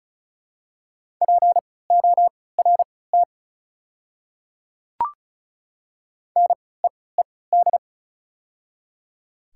Morse code, heard as a keyed tone, about 36 wpm, spaced as for 20 wpm, 700 Hz. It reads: PORT NEED
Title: 3x Character Spacing - Top 200 Words - 35wpm